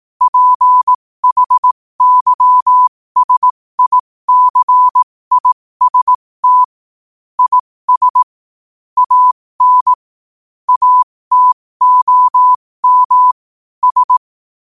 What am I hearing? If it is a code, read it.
Morse code, heard as a keyed tone, about 18 wpm, spaced as for 15 wpm, 1000 Hz. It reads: PHYSICIST IS AN ATOM S
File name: qotdfiles / 15wpm